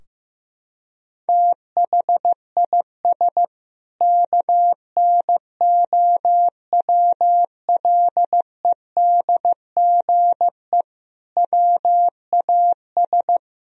Morse code, heard as a keyed tone, about 16 wpm, 700 Hz. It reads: THIS KNOWLEDGE WAS
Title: Sentences from Top 1000 Words - Review Part 3 - 15wpm